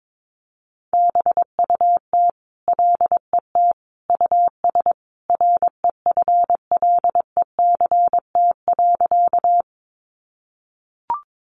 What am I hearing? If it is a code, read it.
Morse code, heard as a keyed tone, about 22 wpm, 700 Hz. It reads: BUT LET US REFLECT.
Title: Sentences from Top 1500 Words 22wpm